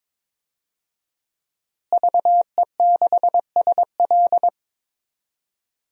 Morse code, heard as a keyed tone, about 22 wpm, 700 Hz. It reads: VE6SL
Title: DXCC-100 Call Signs - 5 Characters - Repeated 2x - 22wpm